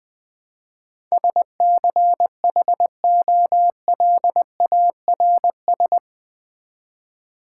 Morse code, heard as a keyed tone, about 20 wpm, 700 Hz. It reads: SCHOLARS
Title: Build A Word - 4 Letters At A Time - 11-Letter Words - 20wpm